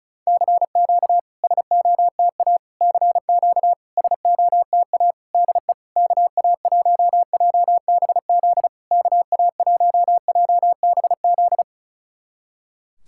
Morse code, heard as a keyed tone, about 36 wpm, 700 Hz. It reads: CQ SOTA CQ SOTA DE KA1JBZ KA1JBZ
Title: SOTA QSO Practice 35wpm